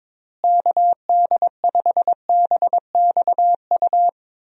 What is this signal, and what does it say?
Morse code, heard as a keyed tone, about 22 wpm, 700 Hz. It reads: KD5BXU
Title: USA Call Signs - 6 Characters - Repeated 3x - 22wpm